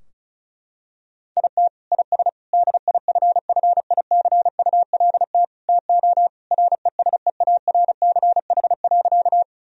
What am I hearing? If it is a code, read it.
Morse code, heard as a keyed tone, about 34 wpm, 700 Hz. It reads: IT IS DIFFICULT TO RESEARCH.
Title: Sentences from Top 1000 Words - Review Part 4 - 35wpm